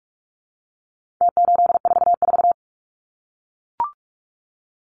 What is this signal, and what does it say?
Morse code, heard as a keyed tone, about 44 wpm, 700 Hz. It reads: T844
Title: Numbers - Contest Style - 4-Digit 45wpm